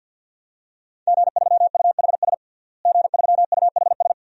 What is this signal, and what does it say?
Morse code, heard as a keyed tone, about 47 wpm, 700 Hz. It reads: G3UHS G3UHS
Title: DXCC-100 Call Signs - 5 Characters - Repeated 2x - 50wpm